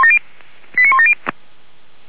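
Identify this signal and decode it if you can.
Five-tone selective call (ZVEI), heard as quick runs of five tones, tones of about 70 ms. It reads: 89170
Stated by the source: ZVEI five-tone selective call 89138, received by SDR, from the feuerwehr-orlamuende.org (Feuerwehr Orlamünde) server.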